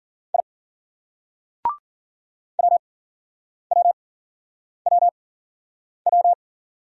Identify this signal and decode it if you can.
Morse code, heard as a keyed tone, about 51 wpm, spaced as for 9 wpm, 700 Hz. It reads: I WWWW